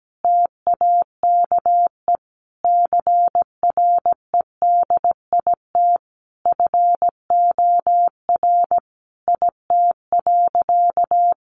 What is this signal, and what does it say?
Morse code, heard as a keyed tone, about 17 wpm, 700 Hz. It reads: TAKE CREDIT FOR IT.
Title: Sentences from Top 1000 Words 17wpm